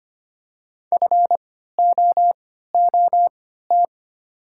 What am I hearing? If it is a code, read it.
Morse code, heard as a keyed tone, about 25 wpm, spaced as for 14 wpm, 700 Hz. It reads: FOOT
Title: 3x Character Spacing - Top 500 Words - 25wpm